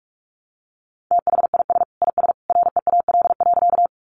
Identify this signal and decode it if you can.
Morse code, heard as a keyed tone, about 44 wpm, 700 Hz. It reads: THIS IS REAL.
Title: Sentences from Top 300 Words 45wpm